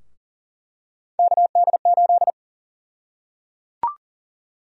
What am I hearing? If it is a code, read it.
Morse code, heard as a keyed tone, about 37 wpm, 700 Hz. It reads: KD8
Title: USA Call Sign Prefixes - 40wpm